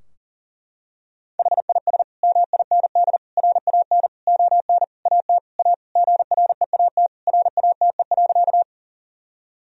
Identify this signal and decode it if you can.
Morse code, heard as a keyed tone, about 40 wpm, 700 Hz. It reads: HIS MIND RAN ON AT A GREAT RATE.
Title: Sentences from Top 1000 Words - Review Part 4 - 40wpm